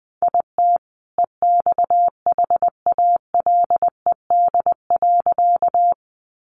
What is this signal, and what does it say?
Morse code, heard as a keyed tone, about 20 wpm, 700 Hz. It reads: IT EXHALED.
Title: A Sound Of Thunder 20wpm